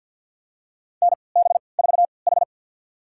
Morse code, heard as a keyed tone, about 51 wpm, spaced as for 28 wpm, 700 Hz. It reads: NB4H
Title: USA Call Signs - 4 Characters - 3x Character Spacing - 50wpm